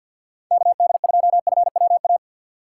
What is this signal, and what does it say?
Morse code, heard as a keyed tone, about 52 wpm, 700 Hz. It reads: KD2FWA
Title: USA Call Signs - 6 Characters - 6x Character Spacing - 50wpm